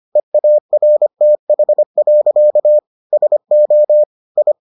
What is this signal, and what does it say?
Morse code, heard as a keyed tone, about 25 wpm, 600 Hz. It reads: EARTH. SO I